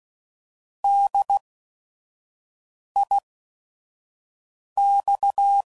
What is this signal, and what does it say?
Morse code, heard as a keyed tone, about 16 wpm, spaced as for 5 wpm, 800 Hz. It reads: DIX
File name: Prefixes / By First Letter / Ds